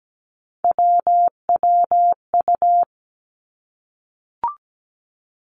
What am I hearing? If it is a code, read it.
Morse code, heard as a keyed tone, about 17 wpm, 700 Hz. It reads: WWU
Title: Three Challenging Letter-Number Combinations 17wpm